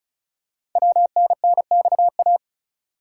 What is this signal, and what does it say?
Morse code, heard as a keyed tone, about 36 wpm, 700 Hz. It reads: WNNXA
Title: Code Groups - 5 Letters - 35wpm